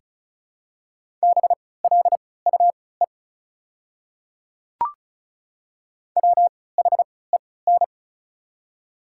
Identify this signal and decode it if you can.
Morse code, heard as a keyed tone, about 36 wpm, spaced as for 20 wpm, 700 Hz. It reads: BLUE WHEN
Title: 3x Character Spacing - Top 500 Words - 35wpm